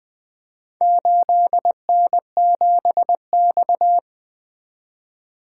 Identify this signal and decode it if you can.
Morse code, heard as a keyed tone, about 20 wpm, 700 Hz. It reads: 8N7X
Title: Four Letter-Number Combinations - 20wpm